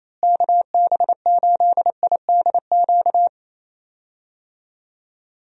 Morse code, heard as a keyed tone, about 28 wpm, 700 Hz. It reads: KB8IDQ
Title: USA Call Signs - 6 Characters - Repeated 3x - 28wpm